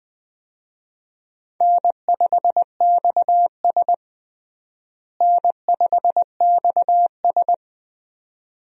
Morse code, heard as a keyed tone, about 20 wpm, 700 Hz. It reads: N5XS N5XS
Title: USA Call Signs - 4 Characters - Repeated 2x - 20wpm